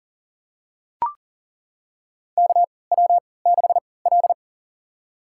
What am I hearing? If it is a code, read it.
Morse code, heard as a keyed tone, about 40 wpm, spaced as for 23 wpm, 700 Hz. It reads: KW6L
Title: USA Call Signs - 3x Character Spacing - 40wpm